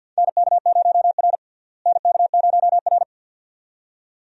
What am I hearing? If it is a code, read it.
Morse code, heard as a keyed tone, about 50 wpm, 700 Hz. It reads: NK0R NK0R
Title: USA Call Signs - 4 Characters - Repeated 2x - 50wpm